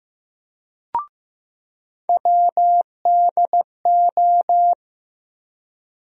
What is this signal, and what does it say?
Morse code, heard as a keyed tone, about 15 wpm, 700 Hz. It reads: WDO
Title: USA Call Sign Postfixes - 3 Letters 15wpm